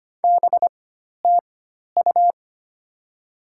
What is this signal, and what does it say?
Morse code, heard as a keyed tone, about 25 wpm, spaced as for 12 wpm, 700 Hz. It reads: BTU
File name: Top 20 QSO Elements - 4x Character Spacing - 25wpm